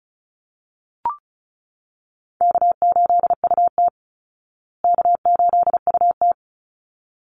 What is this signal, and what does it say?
Morse code, heard as a keyed tone, about 34 wpm, 700 Hz. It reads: K8UT K8UT
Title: Call Signs Repeated 3x - 35wpm